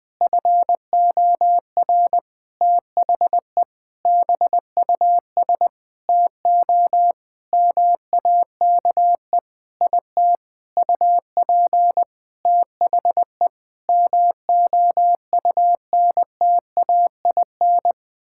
Morse code, heard as a keyed tone, about 21 wpm, 700 Hz. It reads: FOR THE BUS TO MAKE IT UP THE MOUNTAIN
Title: Sentences from Top 1000 Words - Review Part 6 - 20wpm